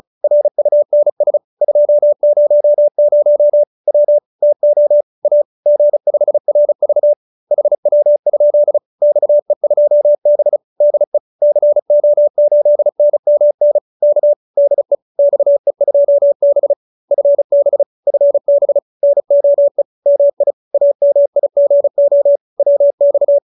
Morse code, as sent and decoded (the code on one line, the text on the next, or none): .-. ..- -. ... / ..--- ----- ----- / .-- / - --- / .- / --. ..... .-. ...- / .... .-- ..--.. / -..- . ..--- -... / -.. / -.-. --- ---.. -. -- -. / -.- / -.. / -..- . ..--- -... / ..-. -... / ..-. -... / -. --- . / -- .. / .- -- .. --. --- / .-- -..-
RUNS 200 W TO A G5RV HW? XE2B DE CO8NMN K DE XE2B FB FB NOE MI AMIGO WX